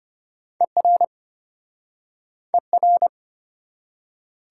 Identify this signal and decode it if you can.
Morse code, heard as a keyed tone, about 33 wpm, 700 Hz. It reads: ER ER